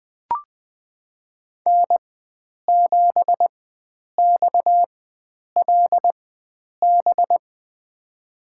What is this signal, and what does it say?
Morse code, heard as a keyed tone, about 21 wpm, spaced as for 9 wpm, 700 Hz. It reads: N7XLB
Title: USA Call Signs - 5 Characters - 4x Character Spacing - 20wpm